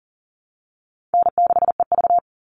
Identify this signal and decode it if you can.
Morse code, heard as a keyed tone, about 40 wpm, 700 Hz. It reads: N6EV
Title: Call Signs - 40wpm